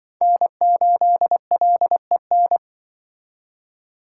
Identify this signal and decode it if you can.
Morse code, heard as a keyed tone, about 25 wpm, 700 Hz. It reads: N8LEN
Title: Speed Racing - USA Call Signs 20wpm